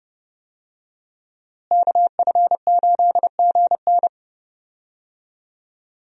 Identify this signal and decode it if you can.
Morse code, heard as a keyed tone, about 30 wpm, 700 Hz. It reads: KF8GN
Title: USA Call Signs - 5 Characters - Repeated 2x - 30wpm